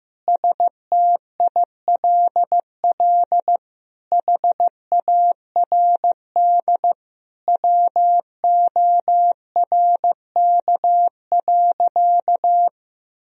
Morse code, heard as a keyed tone, about 15 wpm, 700 Hz. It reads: STILL HARD WORK.
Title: Sentences from Top 1000 Words - Review Part 3 - 15wpm